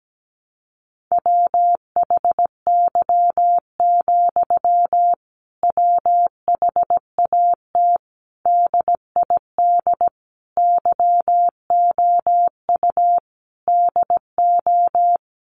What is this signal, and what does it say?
Morse code, heard as a keyed tone, about 17 wpm, 700 Hz. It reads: WHY, WHAT DID YOU DO
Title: Sentences from Top 1500 Words 17wpm